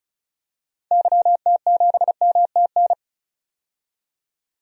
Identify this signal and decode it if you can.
Morse code, heard as a keyed tone, about 36 wpm, 700 Hz. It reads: YT7MTN
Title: DXCC-100 Call Signs - 6 Characters - 35wpm